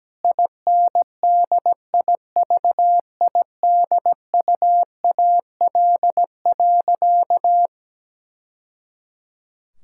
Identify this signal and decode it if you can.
Morse code, heard as a keyed tone, about 17 wpm, 700 Hz. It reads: INDIVIDUAL.
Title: Sentences from Top 1000 Words - Review Part 6 - 17wpm